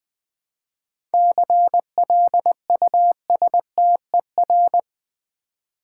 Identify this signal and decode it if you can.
Morse code, heard as a keyed tone, about 20 wpm, 700 Hz. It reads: CLUSTER
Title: 7 Letter Words - First 3 Letters Given - 20wpm